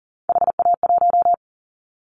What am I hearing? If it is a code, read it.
Morse code, heard as a keyed tone, about 38 wpm, 700 Hz. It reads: HA1
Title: Callsign Prefixes for Top 50 DXCC - 40wpm